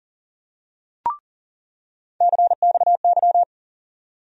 Morse code, heard as a keyed tone, about 40 wpm, 700 Hz. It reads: CXY
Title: USA Call Sign Postfixes - 3 Letters 40wpm